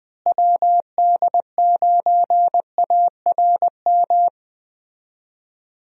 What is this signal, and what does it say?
Morse code, heard as a keyed tone, about 20 wpm, 700 Hz. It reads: WD9ARM
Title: USA Call Signs - 6 Characters - Repeated 2x - 20wpm